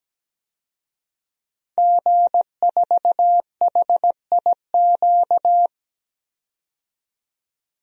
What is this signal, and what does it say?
Morse code, heard as a keyed tone, about 17 wpm, 700 Hz. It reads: G4HIQ
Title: DXCC-100 Call Signs - 5 Characters - 17wpm